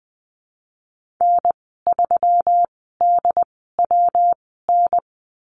Morse code, heard as a keyed tone, about 20 wpm, spaced as for 14 wpm, 700 Hz. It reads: N3DWN
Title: 122 Lesson 20g - Number 3 - Call Signs 20wpm